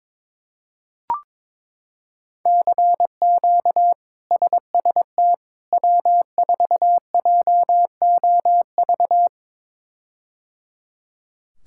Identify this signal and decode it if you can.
Morse code, heard as a keyed tone, about 21 wpm, 700 Hz. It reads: CQ SST W4JOV
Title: K1USN SST QSO Practice - 22wpm